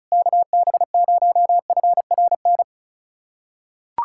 Morse code, heard as a keyed tone, about 36 wpm, 700 Hz. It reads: KB0FRN
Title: USA Callsign DE Callsign - 6 Characters - 35wpm